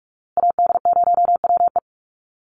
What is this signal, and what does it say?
Morse code, heard as a keyed tone, about 44 wpm, 700 Hz. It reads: AD0WE
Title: Call Signs Repeated 2x - 45wpm